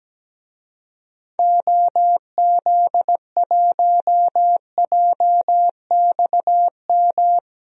Morse code, heard as a keyed tone, about 17 wpm, 700 Hz. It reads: OZ1JXM
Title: DXCC-100 Call Signs - 6 Characters - Repeated 2x - 17wpm